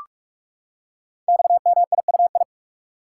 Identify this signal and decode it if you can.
Morse code, heard as a keyed tone, about 44 wpm, 700 Hz. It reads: XMIUI
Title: Code Groups - 5 Letters - 45wpm